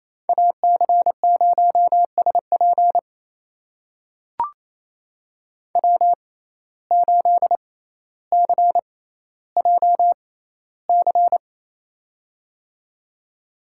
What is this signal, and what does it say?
Morse code, heard as a keyed tone, about 28 wpm, spaced as for 14 wpm, 700 Hz. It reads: AC0SP W8CJC
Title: USA Call Signs - 5 Characters - 6x Character Spacing - 28wpm